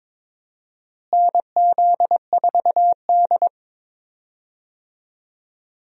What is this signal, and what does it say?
Morse code, heard as a keyed tone, about 22 wpm, 700 Hz. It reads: NZ4D
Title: USA Call Signs - 4 Characters - Repeated 2x - 22wpm